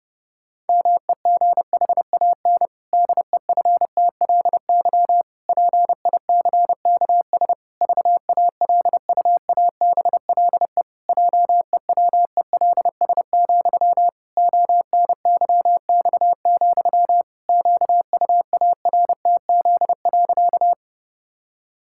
Morse code, heard as a keyed tone, about 30 wpm, 700 Hz. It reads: MEGHAN DEFTLY PICKS VALUABLE JEWELS, ONYX, QUARTZ.
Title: Pangram Challenge v2 - 30wpm